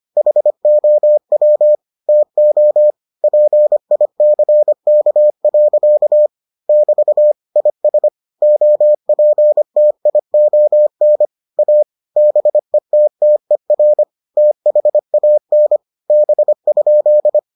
Morse code, heard as a keyed tone, about 25 wpm, 600 Hz. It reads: HOW TO PICK. <BT> IS OPTION A BETTER THAN B?